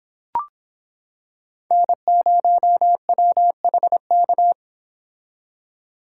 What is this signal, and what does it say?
Morse code, heard as a keyed tone, about 26 wpm, 700 Hz. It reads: N0WHK